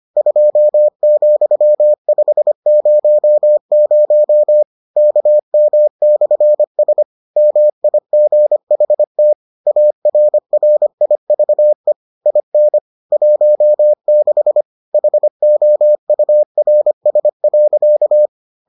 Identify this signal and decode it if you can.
Morse code, heard as a keyed tone, about 25 wpm, 600 Hz. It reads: 2,500 KM/S MIGHT ARRIVE IN 16 HOURS.